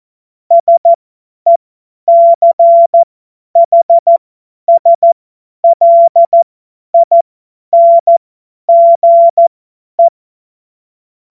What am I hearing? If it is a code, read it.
Morse code, heard as a keyed tone, about 14 wpm, spaced as for 10 wpm, 680 Hz. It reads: SECHSLINGE
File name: Morsekurs / Witze / Witz 05 März 2024 50 F70